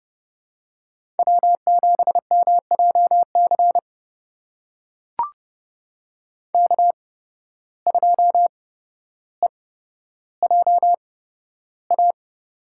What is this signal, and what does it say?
Morse code, heard as a keyed tone, about 31 wpm, spaced as for 13 wpm, 700 Hz. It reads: W7MJC K2EJA
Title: USA Call Signs - 5 Characters - 8x Character Spacing - 30wpm